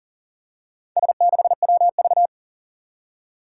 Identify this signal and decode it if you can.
Morse code, heard as a keyed tone, about 42 wpm, 700 Hz. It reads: S6WV
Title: Four Challenging Letter-Number Combinations 40wpm